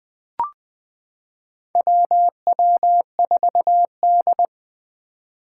Morse code, heard as a keyed tone, about 20 wpm, 700 Hz. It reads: WW4D
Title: 2023 SOTA Activator Call Signs - 20wpm